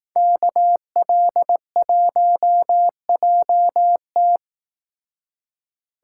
Morse code, heard as a keyed tone, about 18 wpm, 700 Hz. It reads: KL1JT
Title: Speed Racing - USA 2x2 Call Signs 15wpm